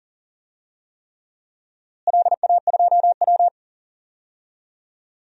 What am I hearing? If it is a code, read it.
Morse code, heard as a keyed tone, about 40 wpm, 700 Hz. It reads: LA2W
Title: DXCC-100 Call Signs - 4 Characters - Repeated 3x - 40wpm